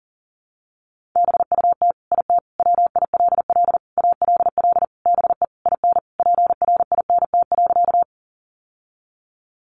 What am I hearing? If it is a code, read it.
Morse code, heard as a keyed tone, about 40 wpm, 700 Hz. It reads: BUT IT WILL ALL BE IN PRINT.
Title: Sentences from Top 1400 Words 40wpm